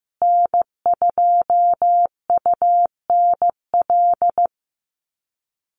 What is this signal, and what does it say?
Morse code, heard as a keyed tone, about 15 wpm, 700 Hz. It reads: N2UNL
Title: USA Call Signs Repeated 2x 15wpm